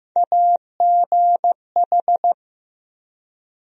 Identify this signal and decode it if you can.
Morse code, heard as a keyed tone, about 16 wpm, 700 Hz. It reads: AGH
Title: USA Call Sign Postfixes - 3 Letters 15wpm